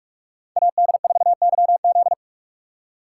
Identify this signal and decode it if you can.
Morse code, heard as a keyed tone, about 46 wpm, 700 Hz. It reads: AD4YZ
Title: USA Call Signs - 5 Characters - 6x Character Spacing - 45wpm